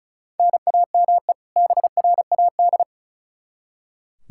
Morse code, heard as a keyed tone, about 34 wpm, 700 Hz. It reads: NAME BRAD
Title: Practice QSO - Exchange 1 - Beginning - 35wpm